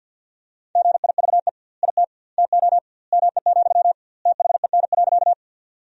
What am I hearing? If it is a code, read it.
Morse code, heard as a keyed tone, about 50 wpm, 700 Hz. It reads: GIVE IT TO ME, THEN.
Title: 2x Word-Spacing - Sentences from Top 200 Words - 50wpm